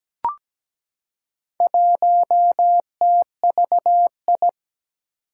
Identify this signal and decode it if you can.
Morse code, heard as a keyed tone, about 17 wpm, 700 Hz. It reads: 1TVI
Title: Four Challenging Letter-Number Combinations 17wpm